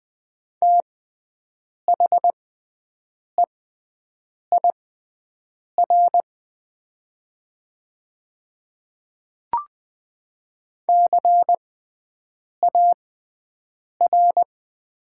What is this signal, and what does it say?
Morse code, heard as a keyed tone, about 20 wpm, spaced as for 7 wpm, 700 Hz. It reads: THEIR CAR
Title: 6x Character Spacing - Top 500 Words - 20wpm